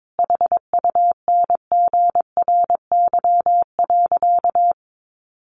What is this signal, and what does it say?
Morse code, heard as a keyed tone, about 22 wpm, 700 Hz. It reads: HUNGRY.
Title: A Sound Of Thunder 22wpm